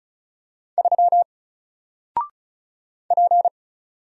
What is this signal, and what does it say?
Morse code, heard as a keyed tone, about 36 wpm, 700 Hz. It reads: 3 P
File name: Single-Letter-Number-35wpm